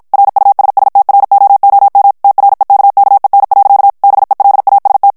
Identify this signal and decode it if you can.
Morse code, heard as a keyed tone, about 50 wpm, 800 Hz. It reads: RUSSTROOM TREKKEN. BELAST
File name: morse mp3 / HOLLANDAIS texte clair 2 vitesse 50 mots minute